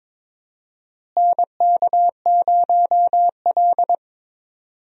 Morse code, heard as a keyed tone, about 22 wpm, 700 Hz. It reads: NK0L
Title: USA Call Signs - 4 Characters - Repeated 3x - 22wpm